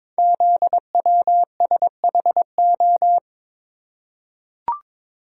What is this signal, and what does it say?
Morse code, heard as a keyed tone, about 22 wpm, 700 Hz. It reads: ZWSHO
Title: Code Groups - 5 Letters - 22wpm